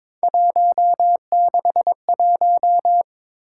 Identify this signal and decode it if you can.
Morse code, heard as a keyed tone, about 22 wpm, 700 Hz. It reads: 161
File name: practice / 22wpm / Three Letter-Number Combinations - 22wpm